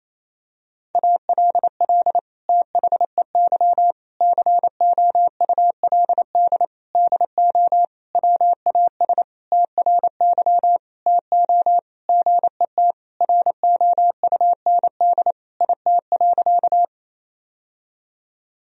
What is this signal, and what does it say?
Morse code, heard as a keyed tone, about 28 wpm, 700 Hz. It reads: ALL THEY COULD DO WAS TRY TO GET ROUND IT.
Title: Sentences from Top 1000 Words - Review Part 1 - 28wpm